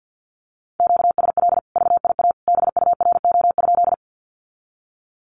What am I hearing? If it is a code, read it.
Morse code, heard as a keyed tone, about 50 wpm, 700 Hz. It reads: QSL VIA BURO?
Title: Common QSO Phrases 50wpm